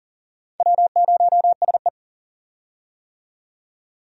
Morse code, heard as a keyed tone, about 39 wpm, 700 Hz. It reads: W0SE
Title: USA Call Signs - 4 Characters - 40wpm